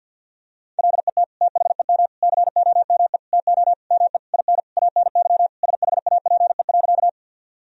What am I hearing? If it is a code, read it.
Morse code, heard as a keyed tone, about 50 wpm, 700 Hz. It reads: LET THEM COME TO ME IN ANY SHAPE.
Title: Sentences from Top 1000 Words - Review Part 3 - 50wpm